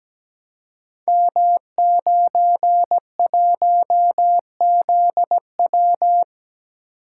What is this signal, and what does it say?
Morse code, heard as a keyed tone, about 17 wpm, 700 Hz. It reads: M91ZW
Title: Code Groups - 5 Letter:Numbers - 17wpm